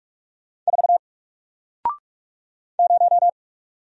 Morse code, heard as a keyed tone, about 44 wpm, 700 Hz. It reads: 4 0
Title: Single Letter-Number Alt 45wpm